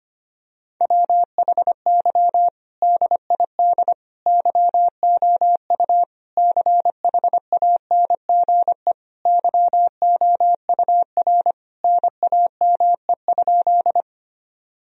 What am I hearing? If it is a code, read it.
Morse code, heard as a keyed tone, about 25 wpm, 700 Hz. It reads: WHY DID YOU CHANGE YOUR NAME?
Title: Sentences from Top 1000 Words - Review Part 1 - 25wpm